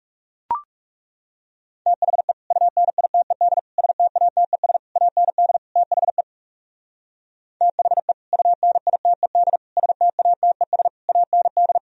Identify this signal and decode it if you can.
Morse code, heard as a keyed tone, about 36 wpm, 700 Hz. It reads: THE UNITED STATES AND THE THE UNITED STATES AND
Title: Speed-Racing - Sets of 5 Words 30wpm